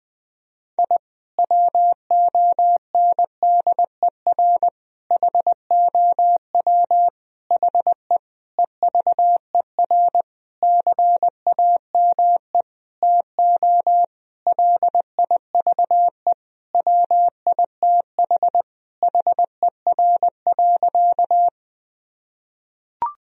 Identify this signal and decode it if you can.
Morse code, heard as a keyed tone, about 20 wpm, 700 Hz. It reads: I WONDER HOW HE EVER CAME TO LIVE WITH HER.
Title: Sentences from Top 1000 Words - Review Part 3 - 20wpm